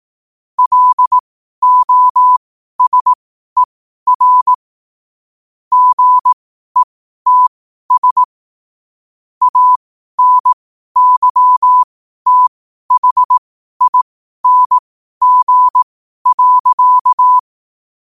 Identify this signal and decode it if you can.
Morse code, heard as a keyed tone, about 18 wpm, spaced as for 12 wpm, 1000 Hz. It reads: LOSER GETS ANYTHING.
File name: qotdfiles / 12wpm